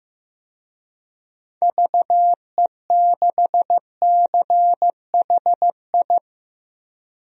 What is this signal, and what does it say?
Morse code, heard as a keyed tone, about 15 wpm, 700 Hz. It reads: VE6CHI
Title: DXCC-100 Call Signs - 6 Characters - Repeated 2x - 15wpm